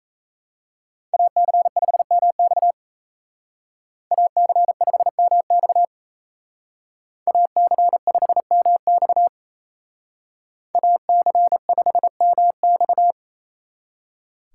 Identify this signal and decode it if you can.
Morse code, heard as a keyed tone, about 36 wpm, 700 Hz. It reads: AC5MX AC5MX AC5MX AC5MX